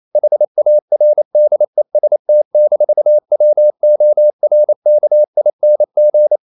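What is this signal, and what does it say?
Morse code, heard as a keyed tone, about 28 wpm, 600 Hz. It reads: HARDEST-WORKING
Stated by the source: Morse code practice at 28 wpm, text of Ken Follett's novel Never in English, from CW med Gnister.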